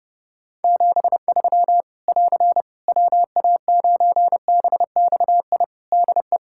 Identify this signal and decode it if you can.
Morse code, heard as a keyed tone, about 30 wpm, 700 Hz. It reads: 73 <AR> WA9BXI DE
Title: K1USN SST QSO Practice 30wpm